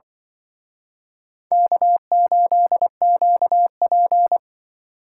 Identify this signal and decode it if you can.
Morse code, heard as a keyed tone, about 25 wpm, 700 Hz. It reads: K8QP